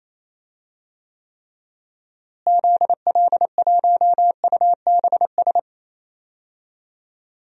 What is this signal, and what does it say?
Morse code, heard as a keyed tone, about 28 wpm, 700 Hz. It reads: ZL1UBS
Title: DXCC-100 Call Signs - 6 Characters - Repeated 3x - 28wpm